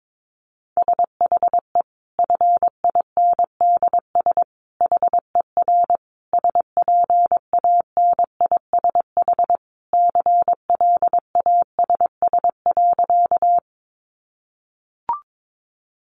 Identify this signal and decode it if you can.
Morse code, heard as a keyed tone, about 22 wpm, 700 Hz. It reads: SHE FINDS HER SPANISH CLASS.
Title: Additional Easy Phrases 22wpm